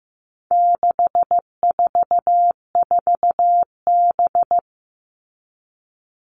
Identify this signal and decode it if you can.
Morse code, heard as a keyed tone, about 15 wpm, 700 Hz. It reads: 644B
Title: All Cut Numbers - 4-Digit 15wpm